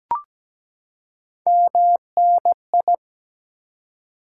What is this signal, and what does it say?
Morse code, heard as a keyed tone, about 17 wpm, 700 Hz. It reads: MNI
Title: Top 90 QSO Elements 17wpm